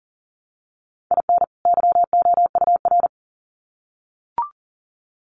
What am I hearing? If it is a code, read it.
Morse code, heard as a keyed tone, about 40 wpm, 700 Hz. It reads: IN YOUR
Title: Sets of 2 Words 40wpm